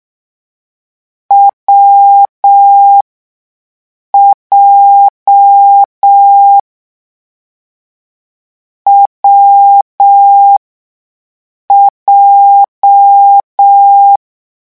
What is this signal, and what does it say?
Morse code, heard as a keyed tone, about 6 wpm, spaced as for 5 wpm, 800 Hz. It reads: WJ WJ
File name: morse mp3 / lecon7-vitesse 6